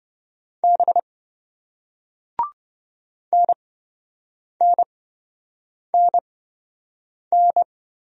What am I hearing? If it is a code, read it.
Morse code, heard as a keyed tone, about 26 wpm, spaced as for 7 wpm, 700 Hz. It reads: B NNNN